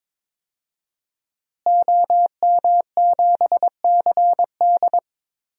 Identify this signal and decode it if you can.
Morse code, heard as a keyed tone, about 22 wpm, 700 Hz. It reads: OM7CD
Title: DXCC-100 Call Signs - 5 Characters - Repeated 3x - 22wpm